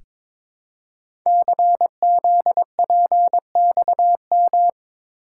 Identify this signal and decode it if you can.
Morse code, heard as a keyed tone, about 22 wpm, 700 Hz. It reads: CZPXM
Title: Code Groups - 5 Letters - 22wpm